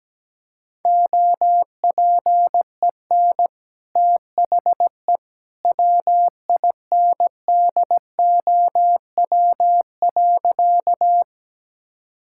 Morse code, heard as a keyed tone, about 17 wpm, 700 Hz. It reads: OPEN THE WINDOW.
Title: Sentences from Top 1600 Words 17wpm